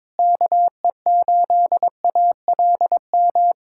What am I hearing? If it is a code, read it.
Morse code, heard as a keyed tone, about 22 wpm, 700 Hz. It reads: KE8ALM
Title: USA Callsign DE Callsign 22wpm